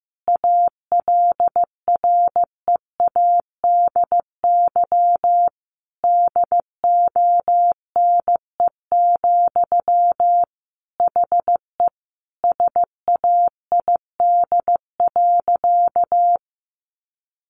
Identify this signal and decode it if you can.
Morse code, heard as a keyed tone, about 15 wpm, 700 Hz. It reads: ALREADY DONE, HE SAID.
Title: Sentences from Top 700 Words 15wpm